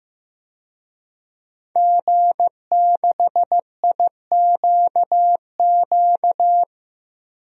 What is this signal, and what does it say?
Morse code, heard as a keyed tone, about 16 wpm, 700 Hz. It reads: G6IQQ
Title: DXCC-100 Call Signs - 5 Characters - 15wpm